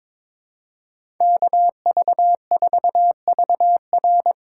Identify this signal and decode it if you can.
Morse code, heard as a keyed tone, about 22 wpm, 700 Hz. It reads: KV4VR
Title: USA Call Signs - 4x Character Spacing - 22wpm